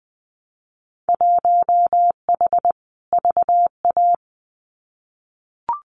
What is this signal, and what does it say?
Morse code, heard as a keyed tone, about 20 wpm, 700 Hz. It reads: 1H VA
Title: Winter Field Day Exchanges 20wpm